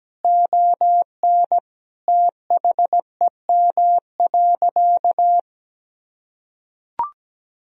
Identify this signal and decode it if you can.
Morse code, heard as a keyed tone, about 17 wpm, 700 Hz. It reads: ON THEM.
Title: Sentences from Top 1000 Words - Review Part 3 - 17wpm